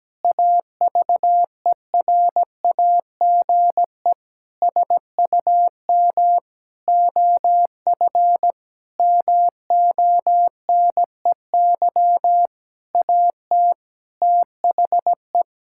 Morse code, heard as a keyed tone, about 17 wpm, 700 Hz. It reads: AVERAGE SUM OF MONEY AT THE
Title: Sentences from Top 1000 Words - Review Part 6 - 17wpm